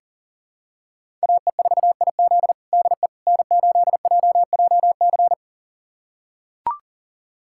Morse code, heard as a keyed tone, about 40 wpm, 700 Hz. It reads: AE4IZ DE N8JJC
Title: USA Callsign DE Callsign - 5 Characters - 40wpm